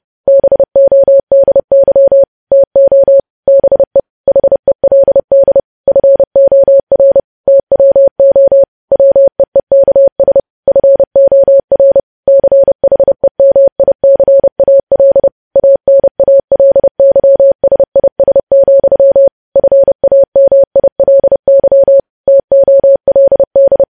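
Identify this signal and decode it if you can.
Morse code, heard as a keyed tone, about 30 wpm, 550 Hz. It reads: BODY TO BE HELD FOR TWO WEEKS FOR CHEMICAL ANALYSIS, FAMILY TOLD
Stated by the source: News headlines sent as Morse practice at 30.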